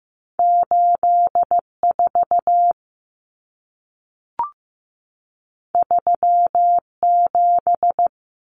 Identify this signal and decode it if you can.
Morse code, heard as a keyed tone, about 15 wpm, 700 Hz. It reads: 8 4 3 7
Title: Numbers - 2-Digit 15wpm